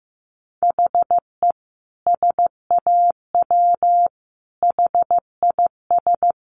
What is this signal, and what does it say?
Morse code, heard as a keyed tone, about 15 wpm, 700 Hz. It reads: HE SAW HIS
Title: Sentences from Top 300 Words 15wpm